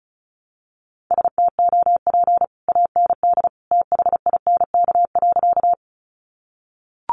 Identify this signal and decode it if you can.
Morse code, heard as a keyed tone, about 34 wpm, 700 Hz. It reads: STOP AND THINK.
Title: Sentences from Top 300 Words 35wpm